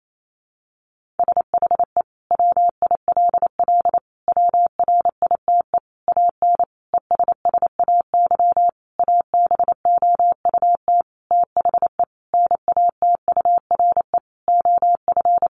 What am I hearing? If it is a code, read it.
Morse code, heard as a keyed tone, about 28 wpm, 700 Hz. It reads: SHE WILL WRITE AN ESSAY ABOUT THE NATURE OF
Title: Sentences from Top 1500 Words 28wpm